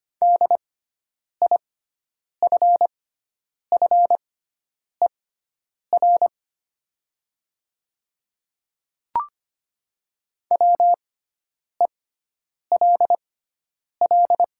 Morse code, heard as a keyed tone, about 25 wpm, spaced as for 9 wpm, 690 Hz. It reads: DIFFER WELL
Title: 6x Character Spacing - Top 200 Words - 25wpm